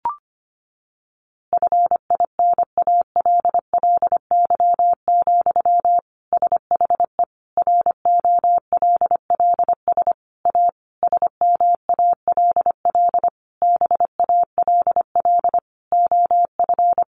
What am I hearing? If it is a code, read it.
Morse code, heard as a keyed tone, about 25 wpm, 700 Hz. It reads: FINALLY, SHE ROLLS A SMALL BALL OF
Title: Easy Phrases 25wpm